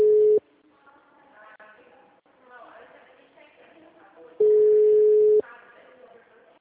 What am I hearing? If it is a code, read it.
Ringback tone (Europe, 425 Hz), with speech.